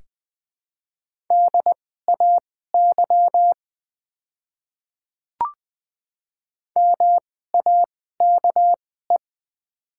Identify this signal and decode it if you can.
Morse code, heard as a keyed tone, about 21 wpm, spaced as for 15 wpm, 700 Hz. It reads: DAY MAKE
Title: 2x Character Spacing - Top 100 Words - 20wpm